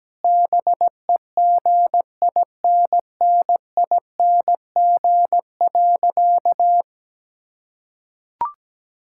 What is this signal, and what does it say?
Morse code, heard as a keyed tone, about 17 wpm, 700 Hz. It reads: BEGINNING.
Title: Sentences from Top 1000 Words - Review Part 5 - 17wpm